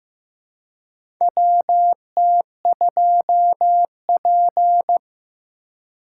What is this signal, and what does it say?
Morse code, heard as a keyed tone, about 15 wpm, 700 Hz. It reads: WT2P
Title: USA Call Signs - 4 Characters - 3x Character Spacing - 15wpm